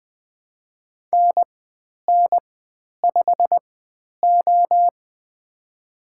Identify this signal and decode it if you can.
Morse code, heard as a keyed tone, about 20 wpm, spaced as for 10 wpm, 700 Hz. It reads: NN5O
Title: 095 Lesson 16g - Letter c - Call Signs 20wpm